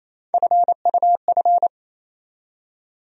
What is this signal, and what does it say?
Morse code, heard as a keyed tone, about 28 wpm, 700 Hz. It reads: FUF
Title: USA Call Sign Postfixes - 3 Letters 28wpm